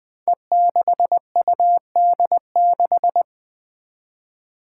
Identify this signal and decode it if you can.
Morse code, heard as a keyed tone, about 20 wpm, 700 Hz. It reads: E6UD6
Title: Five Challenging Letter-Number Combinations 20wpm